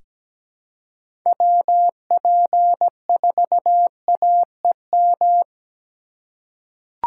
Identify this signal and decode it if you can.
Morse code, heard as a keyed tone, about 17 wpm, 700 Hz. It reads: WP4AEM
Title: USA Call Signs - 6 Characters - 2x Character Spacing - 17wpm